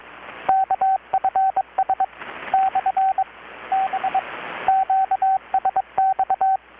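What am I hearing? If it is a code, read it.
Morse code, heard as a keyed tone, about 23 wpm, 770 Hz, with radio noise. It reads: KFS / B QSX
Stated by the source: Morse from KPH's Night of Nights XX maritime radiotelegraphy event, received in Chiapas, Mexico about 3,800 km away on a Kenwood R-600 with a grounded 10 m long-wire antenna.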